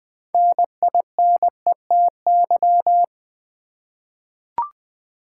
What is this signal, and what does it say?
Morse code, heard as a keyed tone, about 20 wpm, 700 Hz. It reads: NINETY
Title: 6 Letter Words - First 3 Letters Given - 20wpm